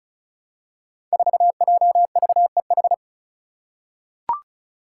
Morse code, heard as a keyed tone, about 34 wpm, 700 Hz. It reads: 4JVEH
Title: Five Challenging Letter-Number Combinations 35wpm